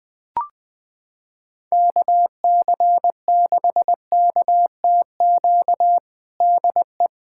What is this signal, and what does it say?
Morse code, heard as a keyed tone, about 19 wpm, 700 Hz. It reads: KC6KTQ DE
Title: USA Callsign DE Callsign 20wpm